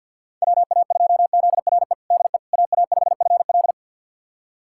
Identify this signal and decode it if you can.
Morse code, heard as a keyed tone, about 46 wpm, 700 Hz. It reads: WA2ZRE DE AA5FL
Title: USA Callsign DE Callsign 50wpm